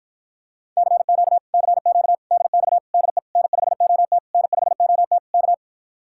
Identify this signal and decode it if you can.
Morse code, heard as a keyed tone, about 54 wpm, 700 Hz. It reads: CQ CQ DX DE N5OT N5OT K